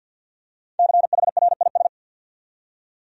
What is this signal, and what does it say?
Morse code, heard as a keyed tone, about 51 wpm, 700 Hz. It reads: CHRIS
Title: USA Common Names - Max 5 Letters - 50wpm